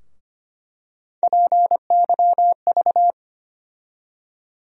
Morse code, heard as a keyed tone, about 25 wpm, 700 Hz. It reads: PYV
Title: USA Call Sign Postfixes - 3 Letters 25wpm